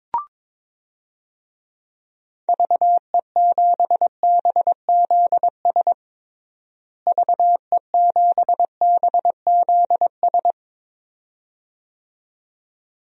Morse code, heard as a keyed tone, about 22 wpm, 700 Hz. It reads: VE7BZS VE7BZS
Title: DXCC-100 Call Signs - 6 Characters - Repeated 2x - 22wpm